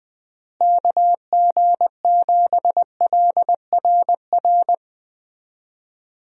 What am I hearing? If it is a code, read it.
Morse code, heard as a keyed tone, about 20 wpm, 700 Hz. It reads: KG7LRR
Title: USA Call Signs - 6 Characters - Repeated 3x - 20wpm